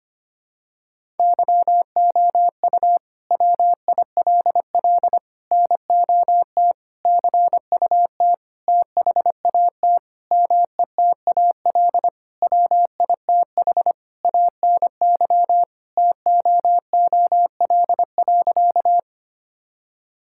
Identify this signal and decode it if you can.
Morse code, heard as a keyed tone, about 25 wpm, 700 Hz. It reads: YOU WILL NOT CUT THAT METAL WITH ANY TOOL.
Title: Sentences from Top 1000 Words - Review Part 6 - 25wpm